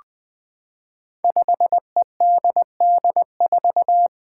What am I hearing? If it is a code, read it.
Morse code, heard as a keyed tone, about 20 wpm, 700 Hz. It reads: 5EDD4
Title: Five Challenging Letter-Number Combinations 20wpm